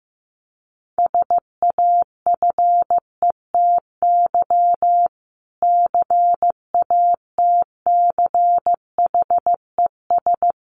Morse code, heard as a keyed tone, about 14 wpm, 700 Hz. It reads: SAFETY CATCHES
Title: A Sound Of Thunder 15wpm